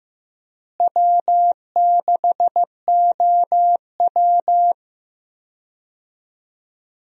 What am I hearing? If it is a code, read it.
Morse code, heard as a keyed tone, about 15 wpm, 700 Hz. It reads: W6OW
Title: USA Call Signs - 4 Characters - Repeated 3x - 15wpm